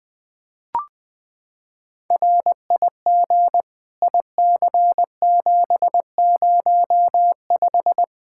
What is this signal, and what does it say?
Morse code, heard as a keyed tone, about 21 wpm, 700 Hz. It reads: RIG IC705
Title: Practice QSO - Exchange 2 - Info - 20wpm